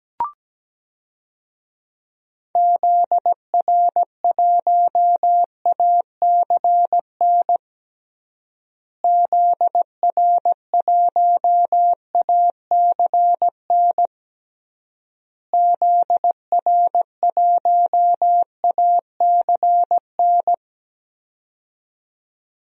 Morse code, heard as a keyed tone, about 17 wpm, 700 Hz. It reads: ZR1ACN ZR1ACN ZR1ACN
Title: DXCC-100 Call Signs - 6 Characters - Repeated 3x - 17wpm